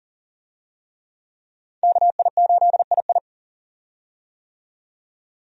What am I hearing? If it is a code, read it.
Morse code, heard as a keyed tone, about 40 wpm, 700 Hz. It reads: KI8II
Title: USA Call Signs - 5 Characters - 40wpm